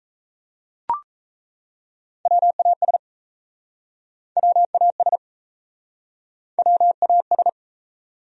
Morse code, heard as a keyed tone, about 39 wpm, 700 Hz. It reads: WAS WAS WAS